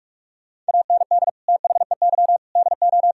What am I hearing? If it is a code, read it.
Morse code, heard as a keyed tone, about 58 wpm, spaced as for 51 wpm, 700 Hz. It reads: AND THEY DO
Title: Speed-Racing - Sets of 4 Words 45wpm